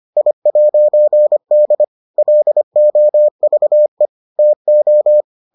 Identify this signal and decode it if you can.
Morse code, heard as a keyed tone, about 25 wpm, 600 Hz. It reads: I'D LOVE TO